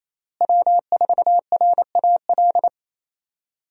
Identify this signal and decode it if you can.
Morse code, heard as a keyed tone, about 28 wpm, 700 Hz. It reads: W4RAL
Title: USA Call Signs - 5 Characters - Repeated 2x - 28wpm